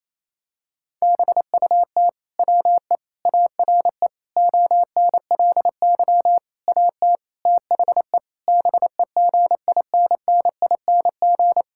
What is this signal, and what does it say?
Morse code, heard as a keyed tone, about 28 wpm, 700 Hz. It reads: BUT WE ARE ONLY AT THE BEGINNING
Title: Sentences from Top 1000 Words - Review Part 5 - 28wpm